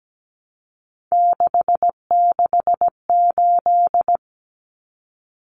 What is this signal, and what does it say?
Morse code, heard as a keyed tone, about 17 wpm, 700 Hz. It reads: 668
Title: Numbers - 3-Digit 17wpm